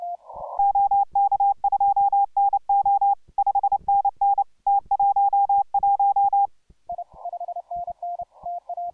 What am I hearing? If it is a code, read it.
Morse code, heard as a keyed tone, about 28 wpm, 810 Hz. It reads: OK2NO 5NN T11